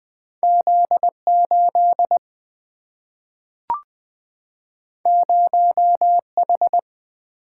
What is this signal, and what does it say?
Morse code, heard as a keyed tone, about 20 wpm, 700 Hz. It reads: Z8 0H
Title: Single-Letter Single-Number 20wpm